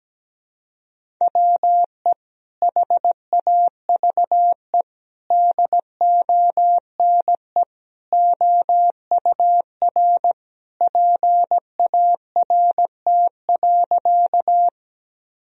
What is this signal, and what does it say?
Morse code, heard as a keyed tone, about 17 wpm, 700 Hz. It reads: WE HAVE DONE OUR PART.
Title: Sentences from Top 1000 Words - Review Part 1 - 17wpm